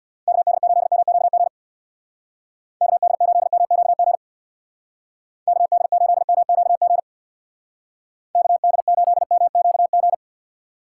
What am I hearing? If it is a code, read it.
Morse code, heard as a keyed tone, about 62 wpm, 700 Hz. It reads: KD8MQG KD8MQG KD8MQG KD8MQG
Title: Speed Racing - USA 2x3 Call Signs 50wpm